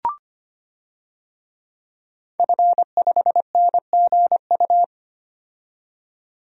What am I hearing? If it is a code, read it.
Morse code, heard as a keyed tone, about 25 wpm, 700 Hz. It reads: F5NGU
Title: DXCC-100 Call Signs - 5 Characters - 25wpm